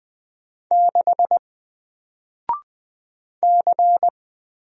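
Morse code, heard as a keyed tone, about 20 wpm, 700 Hz. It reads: 6 C
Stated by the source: Single letter and number Morse drill at 20 wpm, from N4LCD.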